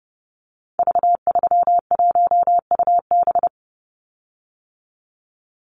Morse code, heard as a keyed tone, about 30 wpm, 700 Hz. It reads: V31UB
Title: Call Signs - 30wpm